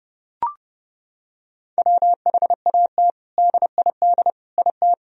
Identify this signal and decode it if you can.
Morse code, heard as a keyed tone, about 30 wpm, 700 Hz. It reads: WHAT DID IT
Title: Sentences from Top 1000 Words - Review Part 3 - 30wpm